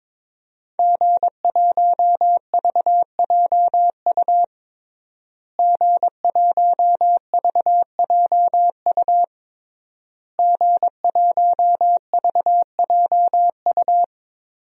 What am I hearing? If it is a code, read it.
Morse code, heard as a keyed tone, about 22 wpm, 700 Hz. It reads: G1VJU G1VJU G1VJU
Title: DXCC-100 Call Signs - 5 Characters - Repeated 3x - 22wpm